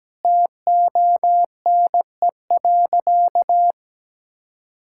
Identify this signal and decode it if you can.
Morse code, heard as a keyed tone, about 17 wpm, 700 Hz. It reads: TONE.
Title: Sentences from Top 1600 Words 17wpm